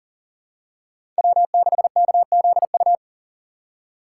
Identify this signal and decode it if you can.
Morse code, heard as a keyed tone, about 40 wpm, 700 Hz. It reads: W6KZU
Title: USA Call Signs - 4x Character Spacing - 40wpm